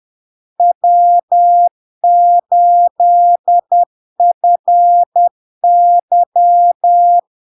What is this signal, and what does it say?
Morse code, heard as a keyed tone, about 10 wpm, 700 Hz. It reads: W8FY
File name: LARK-FD-2022-Log-in-CW-10wpm-700-Hz